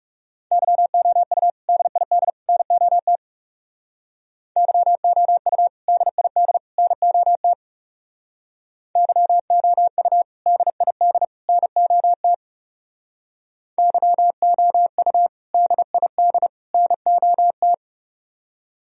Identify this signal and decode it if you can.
Morse code, heard as a keyed tone, about 36 wpm, 700 Hz. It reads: YOU DID NOT YOU DID NOT YOU DID NOT YOU DID NOT
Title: Speed-Racing - Sets of 3 Words 30wpm